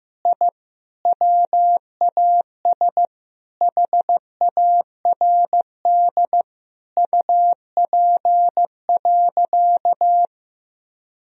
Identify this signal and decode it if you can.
Morse code, heard as a keyed tone, about 16 wpm, 700 Hz. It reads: I WAS HARD UP.
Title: Sentences from Top 1000 Words - Review Part 1 - 15wpm